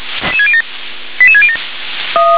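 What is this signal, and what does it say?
Five-tone selective call (ZVEI), heard as quick runs of five tones, tones of about 70 ms. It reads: 88607
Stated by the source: Five-tone ZVEI selective call 88757, received by SDR.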